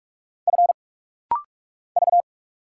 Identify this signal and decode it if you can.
Morse code, heard as a keyed tone, about 48 wpm, 700 Hz. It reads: F V
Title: Single Letter-Number Alt - Rapid-Fire 45wpm